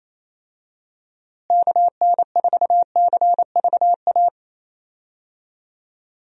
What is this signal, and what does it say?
Morse code, heard as a keyed tone, about 28 wpm, 700 Hz. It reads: KN4CVA
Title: USA Call Signs - 6 Characters - Repeated 2x - 28wpm